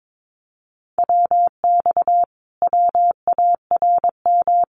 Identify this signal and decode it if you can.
Morse code, heard as a keyed tone, about 22 wpm, 700 Hz. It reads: WX WARM
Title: Common QSO Phrases Encore 22wpm